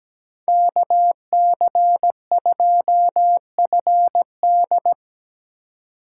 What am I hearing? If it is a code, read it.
Morse code, heard as a keyed tone, about 19 wpm, 700 Hz. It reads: KC2FD
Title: Speed Racing - USA 2x2 Call Signs 17wpm